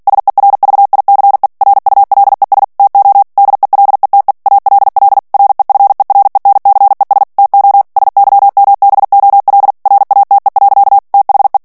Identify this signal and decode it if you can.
Morse code, heard as a keyed tone, about 48 wpm, 800 Hz. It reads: SERVICE RULES TO DELETE ALL REFERENCES TO SYMBOL RATE. THE
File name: ANGLAIS texte clair 3 vitesse 45 mots minute